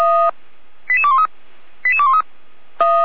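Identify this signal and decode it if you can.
Five-tone selective call (ZVEI), heard as quick runs of five tones, tones of about 70 ms. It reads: 88213 88213
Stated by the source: ZVEI five-tone selective call received on an SDR.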